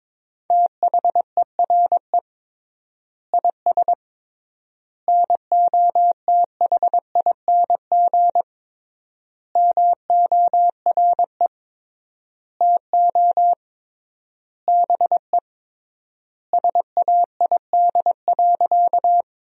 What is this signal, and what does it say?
Morse code, heard as a keyed tone, about 22 wpm, 700 Hz. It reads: THERE IS NOTHING MORE TO BE SAID.
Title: 3x Word-Spacing - Sentences from Top 500 Words - 22wpm